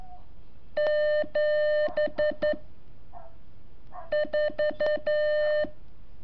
Morse code, heard as a keyed tone, about 12 wpm, 600 Hz. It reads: 7 4
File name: curso / ejercicio 22